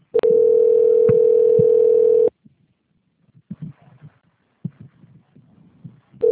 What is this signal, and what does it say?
Ringback tone (North America), 440 + 480 Hz.